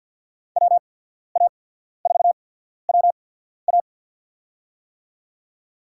Morse code, heard as a keyed tone, about 54 wpm, spaced as for 14 wpm, 700 Hz. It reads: WA4WA
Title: USA Call Signs - 5 Characters - 8x Character Spacing - 50wpm